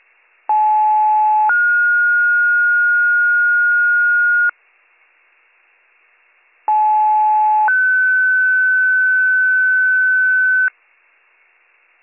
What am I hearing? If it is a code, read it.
Two-tone page, heard as a one-second tone, then a three-second tone, about 860 then 1430 Hz, about 860 then 1500 Hz, with noise. It is heard from 0.5 to 10.7 s.